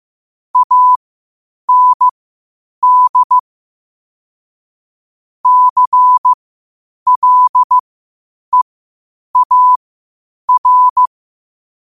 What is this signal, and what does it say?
Morse code, heard as a keyed tone, about 15 wpm, spaced as for 8 wpm, 1000 Hz. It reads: AND CLEAR
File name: qotdfiles / 8wpm